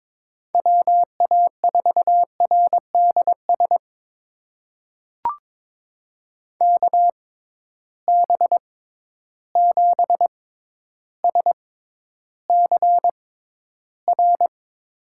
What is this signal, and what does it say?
Morse code, heard as a keyed tone, about 22 wpm, spaced as for 11 wpm, 700 Hz. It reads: WA4RDS KB7SCR
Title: USA Call Signs - 6 Characters - 6x Character Spacing - 22wpm